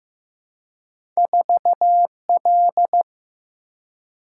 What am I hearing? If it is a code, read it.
Morse code, heard as a keyed tone, about 15 wpm, 700 Hz. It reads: 4 L
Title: Single-Letter Single-Number 15wpm